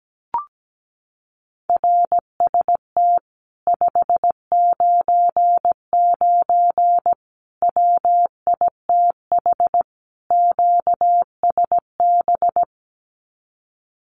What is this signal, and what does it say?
Morse code, heard as a keyed tone, about 17 wpm, 700 Hz. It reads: RST 599 WITH QSB
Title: Common QSO Phrases Encore 17wpm